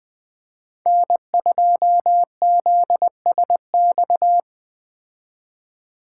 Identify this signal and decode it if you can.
Morse code, heard as a keyed tone, about 20 wpm, 700 Hz. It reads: N2ZSX